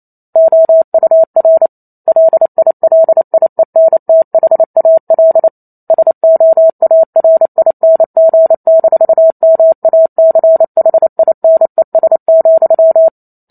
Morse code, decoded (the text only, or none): OUR LILIENTHAL SOARING-MACHINES,